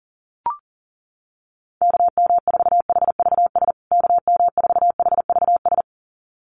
Morse code, heard as a keyed tone, about 40 wpm, 700 Hz. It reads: KM4HVS KM4HVS
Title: USA Call Signs Repeated 2x 40wpm